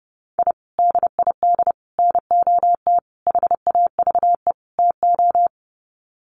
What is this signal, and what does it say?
Morse code, heard as a keyed tone, about 30 wpm, 700 Hz. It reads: I DID NOT HAVE TO
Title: Sets of 5 Words 30wpm